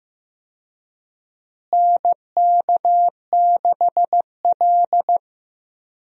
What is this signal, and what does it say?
Morse code, heard as a keyed tone, about 15 wpm, 700 Hz. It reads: NK6L
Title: USA Call Signs - 4 Characters - Repeated 2x - 15wpm